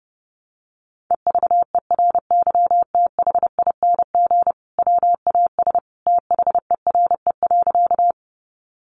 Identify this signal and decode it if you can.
Morse code, heard as a keyed tone, about 30 wpm, 700 Hz. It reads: EVERYTHING WAS THERE.
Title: Sentences from Top 700 Words 30wpm